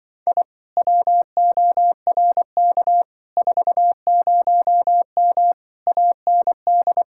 Morse code, heard as a keyed tone, about 24 wpm, 700 Hz. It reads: I WORK 40M AND
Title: Speed-Racing Common QSO Phrases - 20wpm